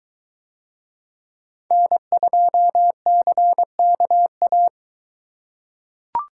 Morse code, heard as a keyed tone, about 22 wpm, 700 Hz. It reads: N2CKA